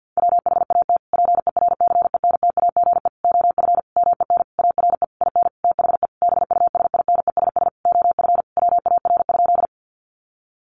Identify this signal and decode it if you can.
Morse code, heard as a keyed tone, about 50 wpm, 700 Hz. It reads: WHAT PERCENTAGE OF MEN ARE IN THE BUSINESS OF WAR?
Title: Sentences from Top 1300 Words 50wpm